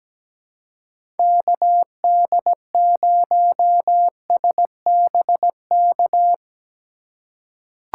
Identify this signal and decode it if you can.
Morse code, heard as a keyed tone, about 17 wpm, 700 Hz. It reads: KD0SBK
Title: USA Call Signs - 6 Characters - 3x Character Spacing - 17wpm